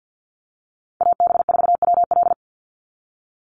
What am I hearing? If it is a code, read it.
Morse code, heard as a keyed tone, about 49 wpm, 700 Hz. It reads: AB4WL
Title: NA SOTA Chaser Callsigns 50wpm